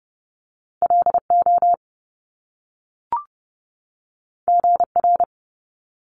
Morse code, heard as a keyed tone, about 30 wpm, 700 Hz. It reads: LO GR
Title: Top Two Letter Combinations in Words 30wpm